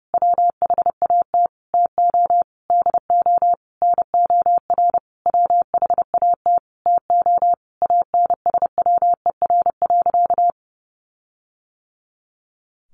Morse code, heard as a keyed tone, about 30 wpm, 700 Hz. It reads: WHAT TO DO NOR WHAT TO ANSWER.
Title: Sentences from Top 1500 Words 30wpm